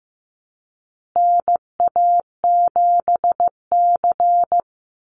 Morse code, heard as a keyed tone, about 15 wpm, 700 Hz. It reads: NA7C
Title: Call Signs Repeated 2x - 15wpm